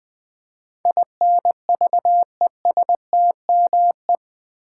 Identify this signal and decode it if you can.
Morse code, heard as a keyed tone, about 20 wpm, 700 Hz. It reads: INVESTME
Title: Build A Word - 4 Letters At A Time - 10-Letter Words - 20wpm